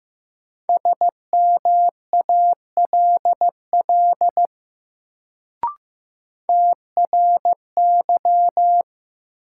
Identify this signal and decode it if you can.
Morse code, heard as a keyed tone, about 15 wpm, 700 Hz. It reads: SMALL TRY
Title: Sentences from Top 200 Words - New Word Drill - 15wpm